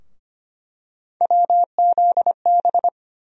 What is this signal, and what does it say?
Morse code, heard as a keyed tone, about 25 wpm, 700 Hz. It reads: WZB
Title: USA Call Sign Postfixes - 3 Letters 25wpm